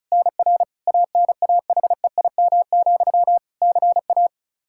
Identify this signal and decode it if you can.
Morse code, heard as a keyed tone, about 36 wpm, 700 Hz. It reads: NR ANAHEIM, CA
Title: Practice QSO - Exchange 1 - Beginning - 35wpm